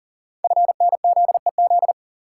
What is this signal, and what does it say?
Morse code, heard as a keyed tone, about 39 wpm, 690 Hz. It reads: FNZEZ
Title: Code Groups - 5 Letters - 40wpm